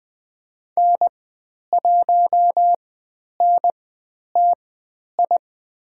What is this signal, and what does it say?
Morse code, heard as a keyed tone, about 20 wpm, spaced as for 10 wpm, 700 Hz. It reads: N1NTI
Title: 109 Lesson 18g - Letter m - Call Signs 20wpm